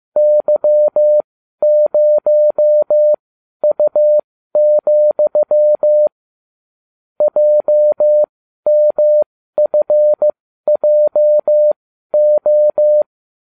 Morse code, heard as a keyed tone, about 15 wpm, spaced as for 12 wpm, 600 Hz. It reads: Y0U, JMFJO
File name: cw / ubacw20 3